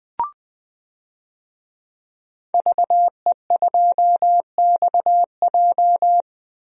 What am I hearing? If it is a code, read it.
Morse code, heard as a keyed tone, about 21 wpm, 700 Hz. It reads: VE2XJ
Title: DXCC-100 Call Signs - 5 Characters - Repeated 3x - 20wpm